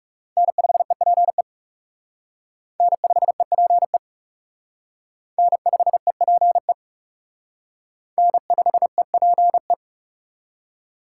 Morse code, heard as a keyed tone, about 37 wpm, 700 Hz. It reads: N5EPE N5EPE N5EPE N5EPE
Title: Speed Racing - USA 1x3 Call Signs 30wpm